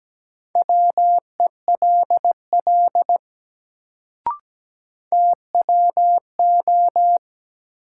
Morse code, heard as a keyed tone, about 17 wpm, 700 Hz. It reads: WELL TWO
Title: Sets of 2 Words - New Word Drill - 17wpm